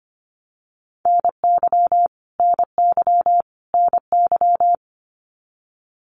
Morse code, heard as a keyed tone, about 25 wpm, 700 Hz. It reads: NY NY NY
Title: US States Repeated 3x 25wpm